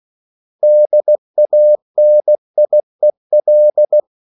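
Morse code, heard as a keyed tone, about 16 wpm, 600 Hz. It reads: DANIEL
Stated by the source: Slow (QRS) Morse code at 16 wpm, from CW med Gnister.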